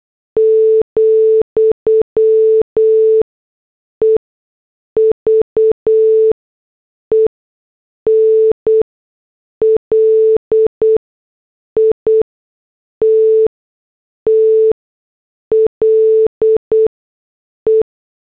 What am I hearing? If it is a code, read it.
Morse code, heard as a keyed tone, about 8 wpm, spaced as for 6 wpm, 440 Hz. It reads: , EVEN LITTLE
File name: morse / headlines